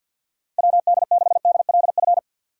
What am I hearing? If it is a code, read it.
Morse code, heard as a keyed tone, about 51 wpm, 700 Hz. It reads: WD6DLF
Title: USA Call Signs - 6 Characters - 50wpm